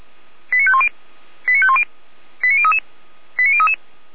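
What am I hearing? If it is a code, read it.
Five-tone selective call (ZVEI), heard as quick runs of five tones, tones of about 70 ms. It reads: 89510 89510 89033 89033
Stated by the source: ZVEI five-tone selective call received on an SDR, from the Feuerwehr Orlamünde audio archive.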